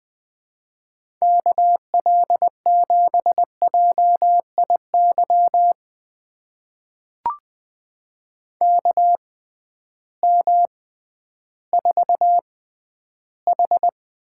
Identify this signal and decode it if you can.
Morse code, heard as a keyed tone, about 20 wpm, spaced as for 12 wpm, 700 Hz. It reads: KL7JIY KM4H
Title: USA Call Signs - 6 Characters - 6x Character Spacing - 20wpm